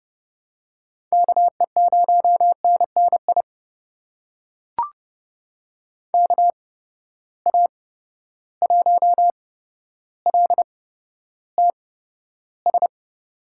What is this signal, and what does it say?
Morse code, heard as a keyed tone, about 32 wpm, spaced as for 13 wpm, 700 Hz. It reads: KE0NNI KA1LTS
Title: USA Call Signs - 6 Characters - 8x Character Spacing - 30wpm